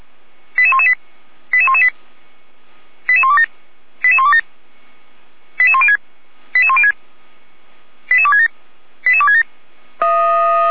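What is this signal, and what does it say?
Five-tone selective call (ZVEI), heard as quick runs of five tones, tones of about 70 ms. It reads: 88118 88118 88127 88127 88106 88106 88267 88267
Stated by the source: ZVEI five-tone call received by SDR.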